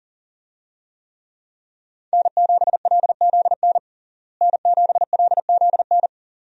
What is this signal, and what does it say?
Morse code, heard as a keyed tone, about 40 wpm, 700 Hz. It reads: N7LZN N7LZN
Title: USA Call Signs - 5 Characters - Repeated 2x - 40wpm